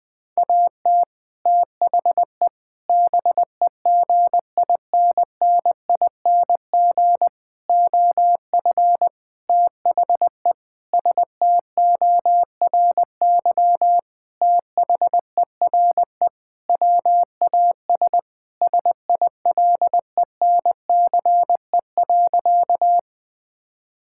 Morse code, heard as a keyed tone, about 20 wpm, 700 Hz. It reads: AT THE BEGINNING OF THE STORY THERE WAS SILENCE.
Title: Sentences with 5-Word Phrases - 20wpm